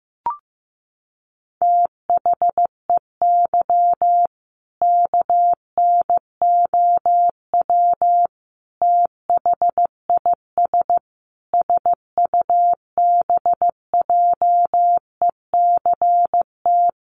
Morse code, heard as a keyed tone, about 15 wpm, 700 Hz. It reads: THEY KNOW THIS SUBJECT
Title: Sentences from Top 800 Words 15wpm